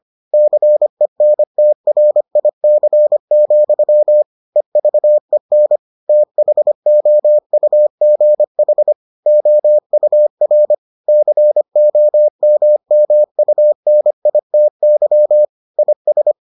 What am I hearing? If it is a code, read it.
Morse code, heard as a keyed tone, about 25 wpm, 600 Hz. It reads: CENTRIC, EVEN THOUGH OUR COMMUNITY IS